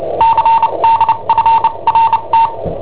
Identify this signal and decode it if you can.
Morse code, heard as a keyed tone, about 29 wpm, spaced as for 25 wpm, 910 Hz. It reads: CDFRT
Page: Cassetta esercizi preparata da I7OHP per IW5ELP